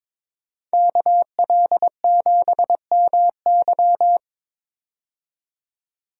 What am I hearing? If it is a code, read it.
Morse code, heard as a keyed tone, about 22 wpm, 700 Hz. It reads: KL7MY
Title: USA Call Signs - 5 Characters - 22wpm